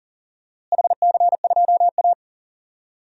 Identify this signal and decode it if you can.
Morse code, heard as a keyed tone, about 40 wpm, 700 Hz. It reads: HC2A
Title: DXCC-100 Call Signs - 4 Characters - Repeated 3x - 40wpm